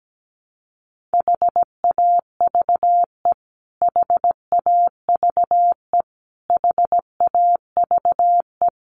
Morse code, heard as a keyed tone, about 17 wpm, 700 Hz. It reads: HAVE HAVE HAVE
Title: Top 500 Words Repeated 17wpm